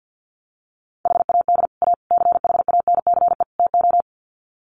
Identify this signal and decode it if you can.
Morse code, heard as a keyed tone, about 50 wpm, 700 Hz. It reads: HAD A CHANCE TO
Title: Sets of 4 Words 50wpm